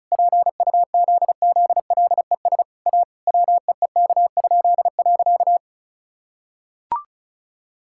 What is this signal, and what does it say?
Morse code, heard as a keyed tone, about 36 wpm, 700 Hz. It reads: PUZZLES A WEEK?.
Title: Pangram Challenge v2 - 35wpm